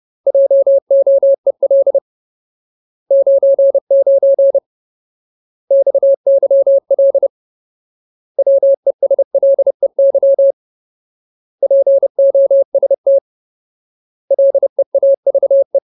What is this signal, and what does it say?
Morse code, heard as a keyed tone, about 30 wpm, 550 Hz. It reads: JOEL 99 XYL WESLEY POST LEAVE
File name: Ultimate Word List / 30WPM Ultimate